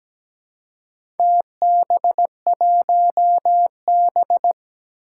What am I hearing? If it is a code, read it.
Morse code, heard as a keyed tone, about 17 wpm, 700 Hz. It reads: TB1B
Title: Four Challenging Letter-Number Combinations 17wpm